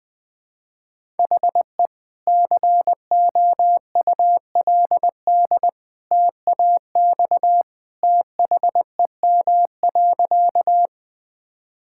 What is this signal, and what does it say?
Morse code, heard as a keyed tone, about 20 wpm, 700 Hz. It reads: HE COULD TAX THEM.
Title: Sentences from Top 1000 Words - Review Part 5 - 20wpm